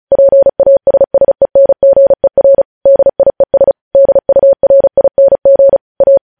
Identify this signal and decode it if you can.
Morse code, heard as a keyed tone, about 36 wpm, 550 Hz. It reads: PASSENGER DIES DURING A
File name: podcast / News headlines-1x-35x35